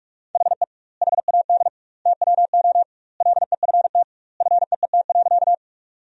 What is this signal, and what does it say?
Morse code, heard as a keyed tone, about 48 wpm, 700 Hz. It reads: HE HAD TWO LEFT FEET.
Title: 2x Word-Spacing - Sentences from Top 300 Words - 45wpm